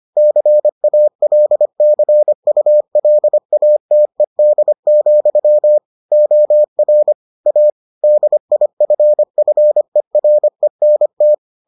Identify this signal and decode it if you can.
Morse code, heard as a keyed tone, about 25 wpm, 600 Hz. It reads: CALCULATED, OR A DIFFERENT